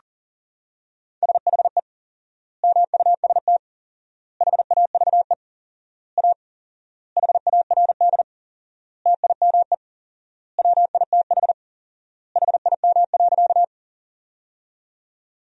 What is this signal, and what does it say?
Morse code, heard as a keyed tone, about 40 wpm, 700 Hz. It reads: SHE MUST HAVE A HARD TIME WITH HIM.
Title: 4x Word-Spacing - Sentences from Top 300 Words - 40wpm